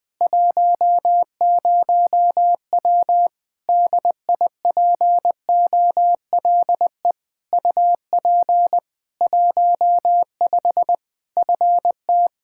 Morse code, heard as a keyed tone, about 20 wpm, 700 Hz. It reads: 10W DIPOLE UP 15 FT
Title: Practice QSO - Exchange 2 - Info - 20wpm